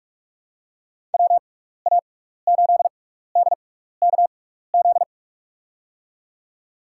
Morse code, heard as a keyed tone, about 46 wpm, spaced as for 15 wpm, 700 Hz. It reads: WA8DKZ
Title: USA Call Signs - 6x Character Spacing - 45wpm